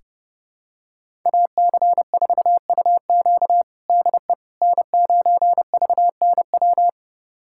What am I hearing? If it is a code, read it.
Morse code, heard as a keyed tone, about 30 wpm, 700 Hz. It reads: AC4UQ DE N9VNW
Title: USA Callsign DE Callsign - 5 Characters - 30wpm